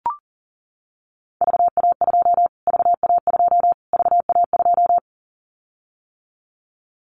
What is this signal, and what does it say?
Morse code, heard as a keyed tone, about 40 wpm, 700 Hz. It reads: VA2 VA2 VA2
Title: Callsign Prefixes for Top 25 DXCC - Repeated 3x - 40wpm